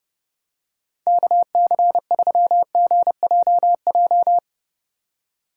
Morse code, heard as a keyed tone, about 32 wpm, 700 Hz. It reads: KC3GJJ
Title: USA Call Signs - 6 Characters - 2x Character Spacing - 30wpm